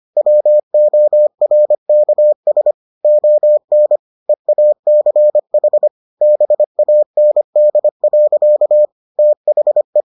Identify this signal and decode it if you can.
Morse code, heard as a keyed tone, about 25 wpm, 600 Hz. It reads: WORKS ON EACH BAND. THE